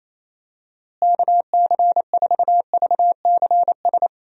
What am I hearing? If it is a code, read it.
Morse code, heard as a keyed tone, about 28 wpm, 700 Hz. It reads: KC4VCS
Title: USA Call Signs - 6 Characters - Repeated 2x - 28wpm